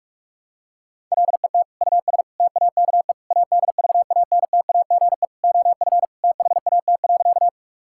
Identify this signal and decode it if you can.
Morse code, heard as a keyed tone, about 46 wpm, 700 Hz. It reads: LET US TAKE ADVANTAGE OF THAT.
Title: Sentences from Top 1000 Words - Review Part 6 - 45wpm